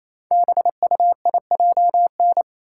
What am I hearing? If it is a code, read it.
Morse code, heard as a keyed tone, about 29 wpm, 700 Hz. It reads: BUIJN
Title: Code Groups - 5 Letters - 28wpm